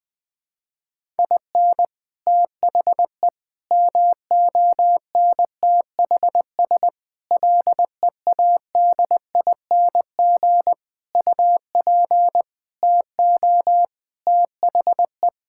Morse code, heard as a keyed tone, about 20 wpm, 700 Hz. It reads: IN THE MONTHS LEADING UP TO THE
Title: Sentences with 6-Word Phrases - 20wpm